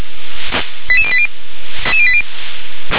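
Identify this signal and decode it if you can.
Five-tone selective call (ZVEI), heard as quick runs of five tones, tones of about 70 ms. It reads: 88080
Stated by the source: Received by SDR as a ZVEI five-tone selective call.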